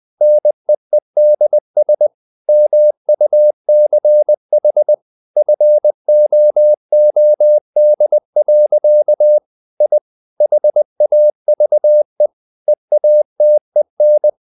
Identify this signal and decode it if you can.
Morse code, heard as a keyed tone, about 20 wpm, 600 Hz. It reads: NEEDS MUCH FOOD. I HAVE EATEN
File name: the old man and the sea part 2 20wpm